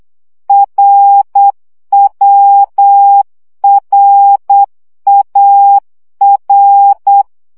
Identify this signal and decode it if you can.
Morse code, heard as a keyed tone, about 8 wpm, 800 Hz. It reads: RWRAR
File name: CW gyakorlas a r w 7 baud